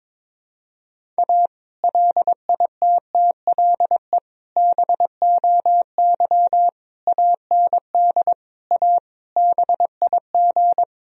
Morse code, heard as a keyed tone, about 22 wpm, 700 Hz. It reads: A LITTLE BOY AND A BIG
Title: Sentences from Top 1000 Words - Review Part 1 - 22wpm